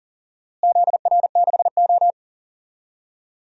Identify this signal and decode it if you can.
Morse code, heard as a keyed tone, about 40 wpm, 700 Hz. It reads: ZR6O
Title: DXCC-100 Call Signs - 4 Characters - Repeated 2x - 40wpm